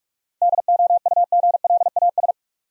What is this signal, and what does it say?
Morse code, heard as a keyed tone, about 46 wpm, 700 Hz. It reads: DOUGLAS
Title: USA Common Names - 45wpm